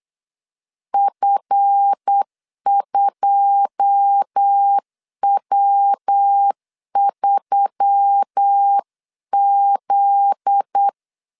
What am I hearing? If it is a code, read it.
Morse code, heard as a keyed tone, about 8 wpm, 800 Hz. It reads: F2W3Z
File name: CW / DET / 1 9 1500C S B 8wpm 5624